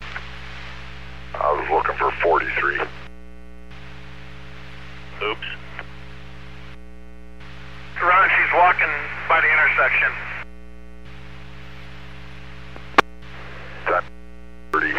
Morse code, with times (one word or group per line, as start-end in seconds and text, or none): none